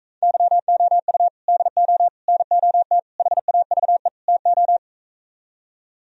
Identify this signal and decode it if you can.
Morse code, heard as a keyed tone, about 42 wpm, 700 Hz. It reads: YOU DO NOT HAVE TO